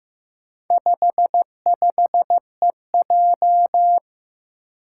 Morse code, heard as a keyed tone, about 15 wpm, 700 Hz. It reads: 55EJ
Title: Four Challenging Letter-Number Combinations 15wpm